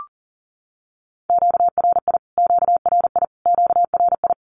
Morse code, heard as a keyed tone, about 40 wpm, 700 Hz. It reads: QRI QRI QRI
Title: Q-Codes Repeated 3x 40wpm